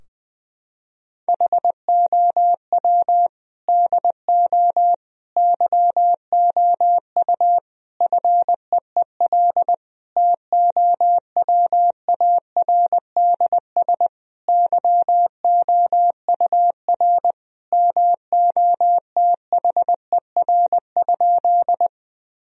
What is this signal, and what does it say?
Morse code, heard as a keyed tone, about 21 wpm, 700 Hz. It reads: HOW DO YOU FEEL TOWARDS YOUR MOTHER?
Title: Sentences from Top 1000 Words - Review Part 2 - 20wpm